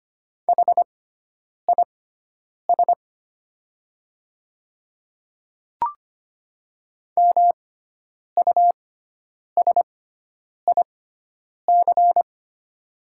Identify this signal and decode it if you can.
Morse code, heard as a keyed tone, about 25 wpm, spaced as for 9 wpm, 700 Hz. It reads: HIS MUSIC
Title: 6x Character Spacing - Top 300 Words - 25wpm